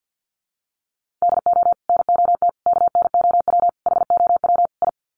Morse code, heard as a keyed tone, about 48 wpm, 700 Hz. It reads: DO NOT KNOW HOW I
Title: Sets of 5 Words Encore2 50wpm